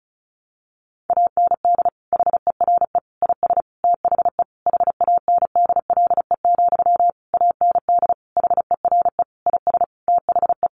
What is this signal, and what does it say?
Morse code, heard as a keyed tone, about 34 wpm, 700 Hz. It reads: AND HERE IS THE HANDLE, AND HERE IS THE
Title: Sentences from Top 1100 Words 35wpm